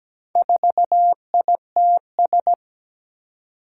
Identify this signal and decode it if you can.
Morse code, heard as a keyed tone, about 17 wpm, 700 Hz. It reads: 4ITS
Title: Four Challenging Letter-Number Combinations 17wpm